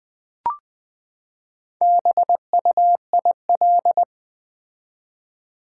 Morse code, heard as a keyed tone, about 20 wpm, 700 Hz. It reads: BUIL
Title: Build A Word - 4 Letters At A Time - 8-Letter Words - 20wpm